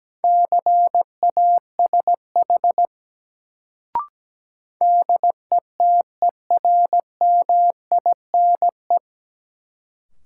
Morse code, heard as a keyed tone, about 17 wpm, 700 Hz. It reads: CASH DETERMINE
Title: Sentences from Top 700 Words - New Word Drill - 17wpm